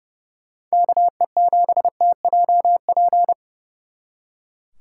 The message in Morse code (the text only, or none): KE7TJP